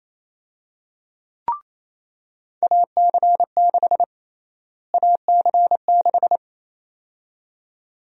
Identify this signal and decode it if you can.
Morse code, heard as a keyed tone, about 28 wpm, 700 Hz. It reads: AC6 AC6
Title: USA Call Sign Prefixes - Repeated 2x - 28wpm